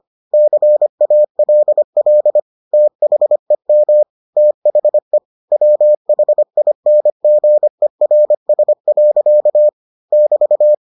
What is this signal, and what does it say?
Morse code, heard as a keyed tone, about 25 wpm, 600 Hz. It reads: CALL THEM THE WHINGERS. <BT>